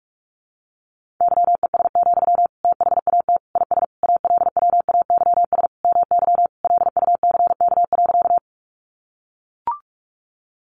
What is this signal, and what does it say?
Morse code, heard as a keyed tone, about 44 wpm, 700 Hz. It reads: YES, THAT IS ALWAYS MY LUCK.
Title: Sentences from Top 1500 Words 45wpm